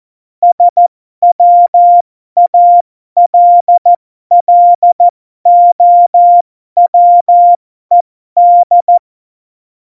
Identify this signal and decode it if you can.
Morse code, heard as a keyed tone, about 14 wpm, spaced as for 12 wpm, 700 Hz. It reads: SWALLOWED
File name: Morsekurs / Witze